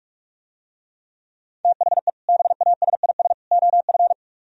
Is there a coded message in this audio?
Yes: Morse code from 1.6 to 4.1 s, heard as a keyed tone, about 44 wpm, 700 Hz. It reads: THE BASIS OF